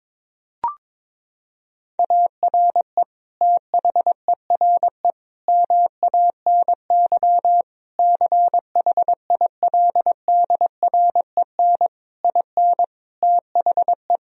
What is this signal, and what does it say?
Morse code, heard as a keyed tone, about 22 wpm, 700 Hz. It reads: ARE THERE MANY CHILDREN IN THE
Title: Sentences from Top 1000 Words - Review Part 4 - 22wpm